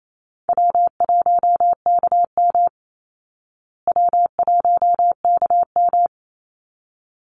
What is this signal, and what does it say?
Morse code, heard as a keyed tone, about 28 wpm, 700 Hz. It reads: W1KM W1KM
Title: Easy Call Signs Repeated 2x - 28wpm